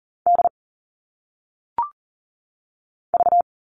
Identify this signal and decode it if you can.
Morse code, heard as a keyed tone, about 40 wpm, 700 Hz. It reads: D V
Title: All Cut Numbers - 1-Digit 40wpm